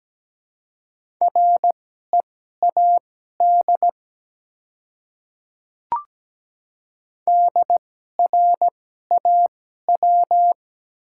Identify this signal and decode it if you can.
Morse code, heard as a keyed tone, about 17 wpm, spaced as for 12 wpm, 700 Hz. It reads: READ DRAW
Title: 2x Character Spacing - Top 300 Words - 17wpm